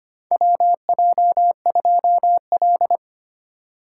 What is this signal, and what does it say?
Morse code, heard as a keyed tone, about 25 wpm, 700 Hz. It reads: WJ2L
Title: K1USN SST QSO Practice - 25wpm